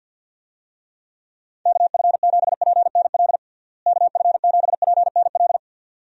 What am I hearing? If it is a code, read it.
Morse code, heard as a keyed tone, about 51 wpm, 700 Hz. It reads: KF7PNL KF7PNL
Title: USA Call Signs - 6 Characters - Repeated 2x - 50wpm